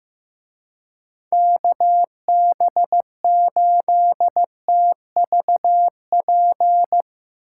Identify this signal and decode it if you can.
Morse code, heard as a keyed tone, about 16 wpm, 700 Hz. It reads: KB8TVP
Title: Speed Racing - USA Call Signs 15wpm